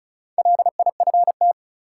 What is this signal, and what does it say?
Morse code, heard as a keyed tone, about 36 wpm, 700 Hz. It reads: LIFT
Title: Sentences from Top 1000 Words - New Word Drill - 35wpm